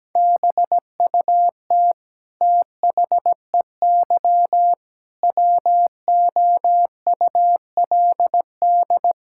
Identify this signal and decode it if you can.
Morse code, heard as a keyed tone, about 17 wpm, 700 Hz. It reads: BUT THEY WOULD
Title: Sentences from Top 1000 Words - Review Part 5 - 17wpm